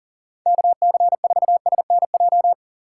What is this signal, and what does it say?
Morse code, heard as a keyed tone, about 37 wpm, 700 Hz. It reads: KC4SNJ
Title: USA Call Signs - 6 Characters - 2x Character Spacing - 40wpm